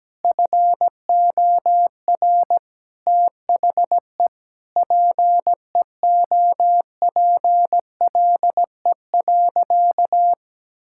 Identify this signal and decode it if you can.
Morse code, heard as a keyed tone, about 17 wpm, 700 Hz. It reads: FOR THE PEOPLE.
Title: Sentences from Top 1000 Words - Review Part 5 - 17wpm